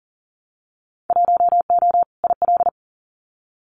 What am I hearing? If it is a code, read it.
Morse code, heard as a keyed tone, about 40 wpm, 700 Hz. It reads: 1O IL
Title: Winter Field Day Exchanges 40wpm